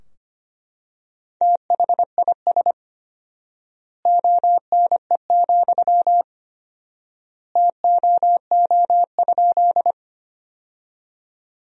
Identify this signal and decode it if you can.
Morse code, heard as a keyed tone, about 25 wpm, 700 Hz. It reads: THIS ONE, TOO?
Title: 4x Word-Spacing - Sentences from Top 200 Words - 25wpm